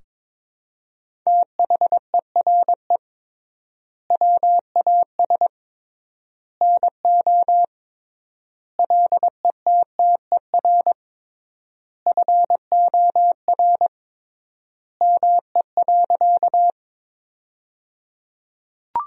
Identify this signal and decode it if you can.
Morse code, heard as a keyed tone, about 22 wpm, 700 Hz. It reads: THERE WAS NO LETTER FOR ME.
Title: 3x Word-Spacing - Sentences from Top 300 Words - 22wpm